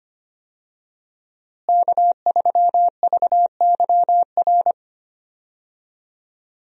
Morse code, heard as a keyed tone, about 25 wpm, 700 Hz. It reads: K3VYR
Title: USA Call Signs - 5 Characters - Repeated 3x - 25wpm